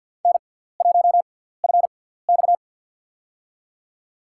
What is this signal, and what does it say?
Morse code, heard as a keyed tone, about 50 wpm, spaced as for 17 wpm, 700 Hz. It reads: N1FX
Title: USA Call Signs - 4 Characters - 6x Character Spacing - 50wpm